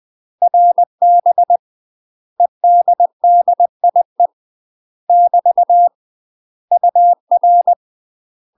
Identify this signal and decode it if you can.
Morse code, heard as a keyed tone, about 20 wpm, 700 Hz. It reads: RB EDDIE <BT> UR
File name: UK MASTER QSOs TRAINING / 20WPM / UK MASTER QSOs 0017